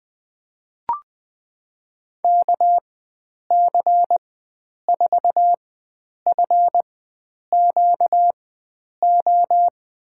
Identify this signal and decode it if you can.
Morse code, heard as a keyed tone, about 20 wpm, spaced as for 9 wpm, 700 Hz. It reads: KC4FQO
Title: USA Call Signs - 6 Characters - 4x Character Spacing - 20wpm